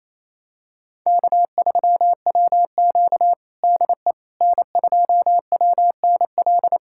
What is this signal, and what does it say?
Morse code, heard as a keyed tone, about 28 wpm, 700 Hz. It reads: K3WQ DE N2WNL
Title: USA Callsign DE Callsign 28wpm